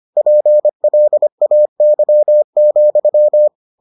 Morse code, heard as a keyed tone, about 25 wpm, 600 Hz. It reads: PLAY,